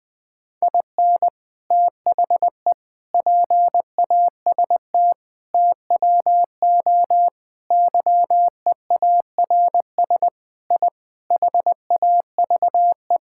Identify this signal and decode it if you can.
Morse code, heard as a keyed tone, about 21 wpm, 700 Hz. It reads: IN THE PAST TWO YEARS I HAVE
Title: Sentences with 5-Word Phrases - 20wpm